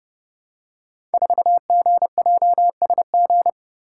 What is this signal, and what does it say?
Morse code, heard as a keyed tone, about 29 wpm, 700 Hz. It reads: 4GJSG
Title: Code Groups - 5 Letter:Numbers - 30wpm